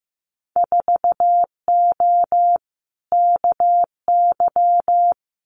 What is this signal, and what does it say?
Morse code, heard as a keyed tone, about 15 wpm, 700 Hz. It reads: 4O KY
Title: Winter Field Day Exchanges Encore 15wpm